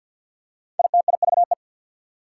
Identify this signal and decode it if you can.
Morse code, heard as a keyed tone, about 48 wpm, 700 Hz. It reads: ITIVE
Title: Prefixes and Suffixes 50wpm